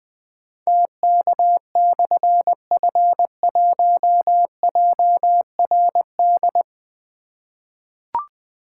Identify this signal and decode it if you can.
Morse code, heard as a keyed tone, about 20 wpm, 700 Hz. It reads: TK/F1JRD
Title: Difficult Call Signs 20wpm